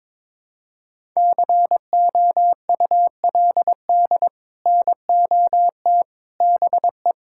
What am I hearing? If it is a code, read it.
Morse code, heard as a keyed tone, about 22 wpm, 700 Hz. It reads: COULD NOT BE
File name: practice / 22wpm / Sets of 6 Words - 22wpm